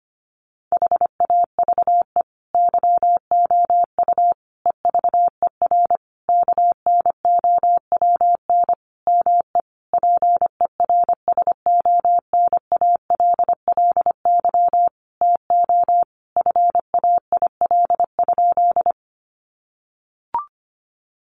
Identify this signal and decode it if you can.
Morse code, heard as a keyed tone, about 25 wpm, 700 Hz. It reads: HAVE YOU EVER KNOWN ME PERSONALLY TO FAIL?
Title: Sentences from Top 1400 Words 25wpm